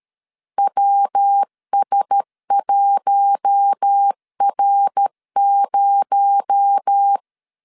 Morse code, heard as a keyed tone, about 13 wpm, 800 Hz. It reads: WS1R0
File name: CW / DET / 1 6 1500C S A 12wpm 3440